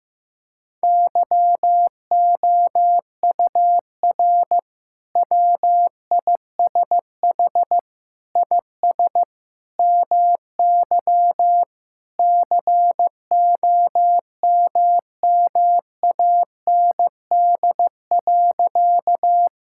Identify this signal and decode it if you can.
Morse code, heard as a keyed tone, about 16 wpm, 700 Hz. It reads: YOUR WISH IS MY COMMAND.
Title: Sentences from Top 1600 Words 15wpm